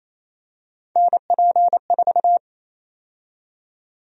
Morse code, heard as a keyed tone, about 28 wpm, 700 Hz. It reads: NP4
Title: USA Call Sign Prefixes - Repeated 2x - 28wpm